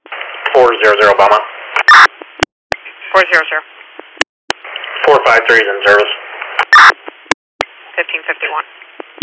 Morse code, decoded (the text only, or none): none